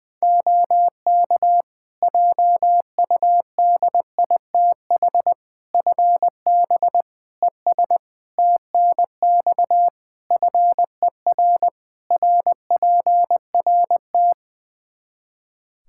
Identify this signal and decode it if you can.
Morse code, heard as a keyed tone, about 21 wpm, 700 Hz. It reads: OK JUDITH FB ES TNX FER RPRT
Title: Practice QSO - Exchange 2 - Info - 20wpm